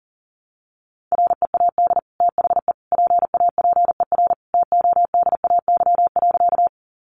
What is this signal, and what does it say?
Morse code, heard as a keyed tone, about 40 wpm, 700 Hz. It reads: READ THE PAPER TODAY.
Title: Sentences from Top 600 Words 40wpm